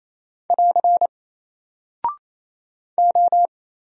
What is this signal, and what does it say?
Morse code, heard as a keyed tone, about 27 wpm, 700 Hz. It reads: <AR> O
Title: Single Character (Letter:Number:Punctuation:Prosign) Alt 28wpm